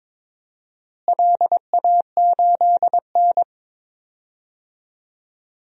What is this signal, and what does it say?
Morse code, heard as a keyed tone, about 22 wpm, 700 Hz. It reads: LA8N
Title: DXCC-100 Call Signs - 4 Characters - Repeated 3x - 22wpm